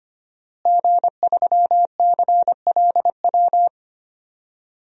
Morse code, heard as a keyed tone, about 25 wpm, 700 Hz. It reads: G3CLW
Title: DXCC-100 Call Signs - 5 Characters - Repeated 3x - 25wpm